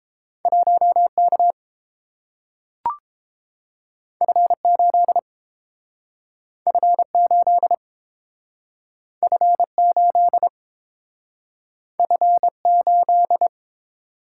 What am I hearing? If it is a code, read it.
Morse code, heard as a keyed tone, about 30 wpm, 700 Hz. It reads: 1K F8 F8 F8 F8